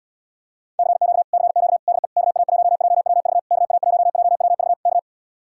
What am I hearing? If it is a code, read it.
Morse code, heard as a keyed tone, about 77 wpm, 700 Hz. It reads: CQ CQ DE KA1JWV KA1JWV K